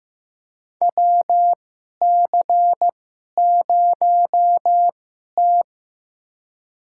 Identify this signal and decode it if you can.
Morse code, heard as a keyed tone, about 15 wpm, spaced as for 11 wpm, 690 Hz. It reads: WC0T
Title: USA Call Signs - 4 Characters - 2x Character Spacing - 15wpm